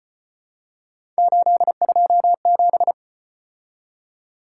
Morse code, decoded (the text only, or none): 827